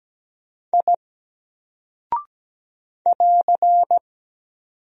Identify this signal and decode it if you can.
Morse code, heard as a keyed tone, about 17 wpm, 700 Hz. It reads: I <AR>
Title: Single Character (Letter:Number:Punctuation:Prosign) 17wpm